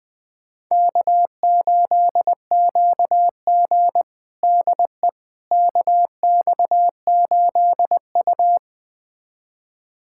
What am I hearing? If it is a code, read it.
Morse code, heard as a keyed tone, about 20 wpm, 700 Hz. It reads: K8QG DE KX8U
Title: USA Callsign DE Callsign - 4 Characters - 20wpm